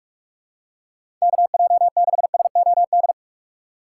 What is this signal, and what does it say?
Morse code, heard as a keyed tone, about 45 wpm, 700 Hz. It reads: KJ6SOD
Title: USA Call Signs - 6 Characters - 3x Character Spacing - 45wpm